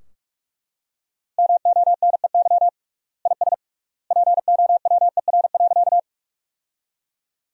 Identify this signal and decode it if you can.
Morse code, heard as a keyed tone, about 43 wpm, 700 Hz. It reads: MONEY IS POWER.
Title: 3x Word-Spacing - Sentences from Top 500 Words - 45wpm